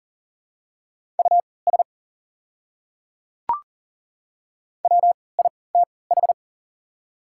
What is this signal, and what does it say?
Morse code, heard as a keyed tone, about 40 wpm, spaced as for 23 wpm, 700 Hz. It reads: US WITH
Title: 3x Character Spacing - Top 200 Words - 40wpm